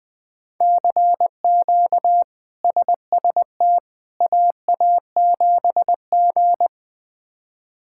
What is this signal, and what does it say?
Morse code, heard as a keyed tone, about 20 wpm, 700 Hz. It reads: CQ SST AA7G
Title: K1USN SST QSO Practice - 20wpm